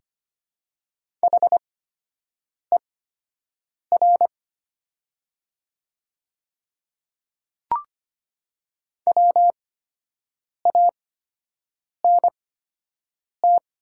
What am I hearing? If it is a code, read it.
Morse code, heard as a keyed tone, about 25 wpm, spaced as for 7 wpm, 700 Hz. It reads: HER WANT
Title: 8x Character Spacing - Top 200 Words - 25wpm